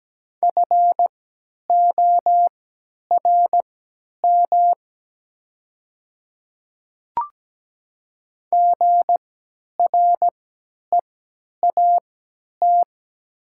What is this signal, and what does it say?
Morse code, heard as a keyed tone, about 17 wpm, spaced as for 10 wpm, 700 Hz. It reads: FORM GREAT
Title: 3x Character Spacing - Top 200 Words - 17wpm